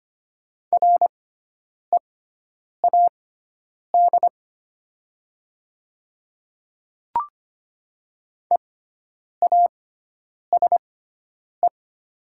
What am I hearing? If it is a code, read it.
Morse code, heard as a keyed tone, about 25 wpm, spaced as for 9 wpm, 700 Hz. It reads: READ EASE
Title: 6x Character Spacing - Top 300 Words - 25wpm